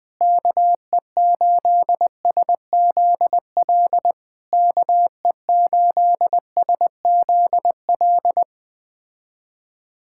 Morse code, heard as a keyed tone, about 20 wpm, 700 Hz. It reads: KE8SZL KE8SZL
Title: POTA QSO Practice - 20wpm